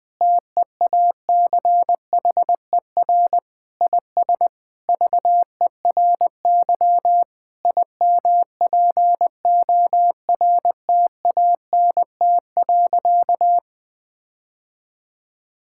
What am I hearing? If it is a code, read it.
Morse code, heard as a keyed tone, about 21 wpm, 700 Hz. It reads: TEACHER IS VERY IMPORTANT.
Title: Sentences with 4-Word Phrases - 20wpm